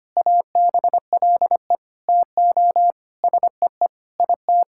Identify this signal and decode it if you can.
Morse code, heard as a keyed tone, about 25 wpm, 700 Hz. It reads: ABLE TO SEE IT
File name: practice / 25wpm / Sentences with 4-Word Phrases - 25wpm